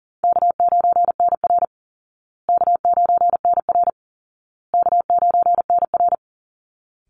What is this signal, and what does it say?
Morse code, heard as a keyed tone, about 40 wpm, 700 Hz. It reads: K9NR K9NR K9NR
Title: Easy Call Signs Repeated 3x - 40wpm